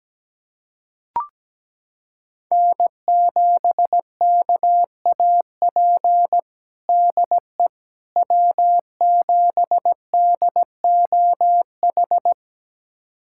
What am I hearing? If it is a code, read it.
Morse code, heard as a keyed tone, about 17 wpm, 700 Hz. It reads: N7KAP DE W7DOH
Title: USA Callsign DE Callsign - 5 Characters - 17wpm